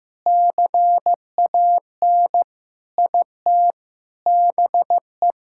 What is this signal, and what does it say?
Morse code, heard as a keyed tone, about 15 wpm, 700 Hz. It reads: CAN IT BE
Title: Sentences from Top 1600 Words 15wpm